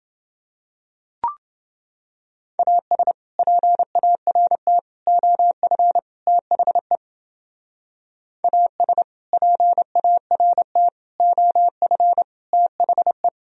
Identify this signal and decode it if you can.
Morse code, heard as a keyed tone, about 29 wpm, 700 Hz. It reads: AS PART OF THE AS PART OF THE